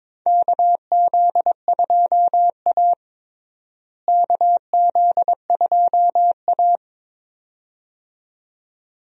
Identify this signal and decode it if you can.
Morse code, heard as a keyed tone, about 22 wpm, 700 Hz. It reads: KZ2A KZ2A
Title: USA Call Signs - 4 Characters - Repeated 2x - 22wpm